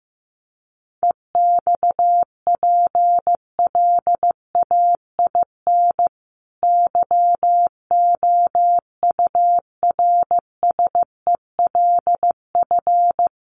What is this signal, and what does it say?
Morse code, heard as a keyed tone, about 15 wpm, 700 Hz. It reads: EXPLAIN YOURSELF
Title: Sentences from Top 1300 Words 15wpm